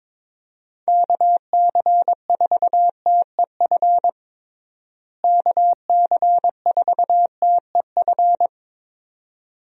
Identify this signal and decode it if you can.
Morse code, heard as a keyed tone, about 21 wpm, 700 Hz. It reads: KC4TEF KC4TEF
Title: USA Call Signs - 6 Characters - Repeated 2x - 22wpm